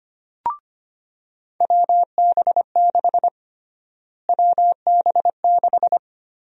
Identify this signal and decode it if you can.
Morse code, heard as a keyed tone, about 25 wpm, 700 Hz. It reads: WB6 WB6
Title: USA Call Sign Prefixes - Repeated 2x - 25wpm